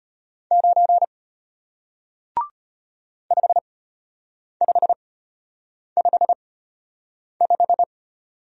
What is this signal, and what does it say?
Morse code, heard as a keyed tone, about 34 wpm, spaced as for 8 wpm, 700 Hz. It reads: 9 5555